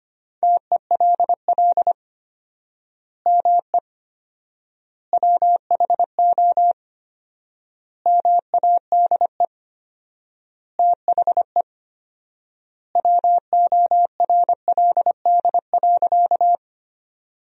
Morse code, heard as a keyed tone, about 25 wpm, 700 Hz. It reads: TELL ME WHO MADE THE WORLD.
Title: 4x Word-Spacing - Sentences from Top 200 Words - 25wpm